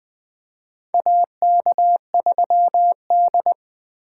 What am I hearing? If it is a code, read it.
Morse code, heard as a keyed tone, about 20 wpm, 700 Hz. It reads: AK3D
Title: USA Call Signs - 4 Characters - 4x Character Spacing - 20wpm